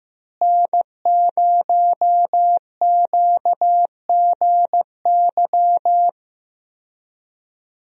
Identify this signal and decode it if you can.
Morse code, heard as a keyed tone, about 15 wpm, 700 Hz. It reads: N0QGY
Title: USA Call Signs - 5 Characters - Repeated 2x - 15wpm